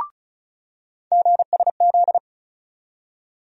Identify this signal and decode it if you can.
Morse code, heard as a keyed tone, about 34 wpm, 700 Hz. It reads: GSZ
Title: USA Call Sign Postfixes - 3 Letters 35wpm